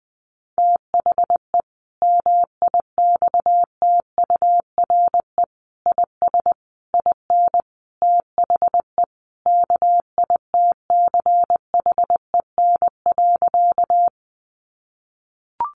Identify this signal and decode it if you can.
Morse code, heard as a keyed tone, about 20 wpm, 700 Hz. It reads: THE MIXTURE IS IN THE KITCHEN.
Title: Sentences from Top 1500 Words 20wpm